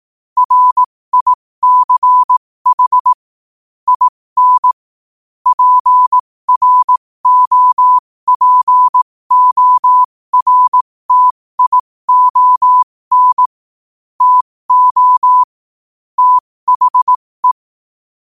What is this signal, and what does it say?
Morse code, heard as a keyed tone, about 18 wpm, spaced as for 15 wpm, 1000 Hz. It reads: RICH IN PROPORTION TO THE
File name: qotdfiles / 15wpm